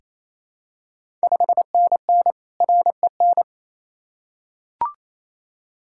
Morse code, heard as a keyed tone, about 27 wpm, 690 Hz. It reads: 5NN REN
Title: Ontario QSO Party Exchange 28wpm